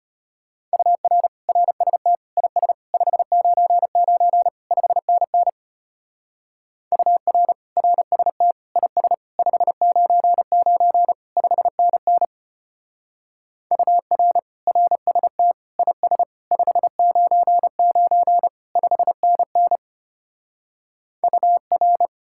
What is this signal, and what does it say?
Morse code, heard as a keyed tone, about 31 wpm, 700 Hz. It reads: UR RST IS 599 5NN UR RST IS 599 5NN UR RST IS 599 5NN UR
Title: Speed-Racing Common QSO Phrases - 25wpm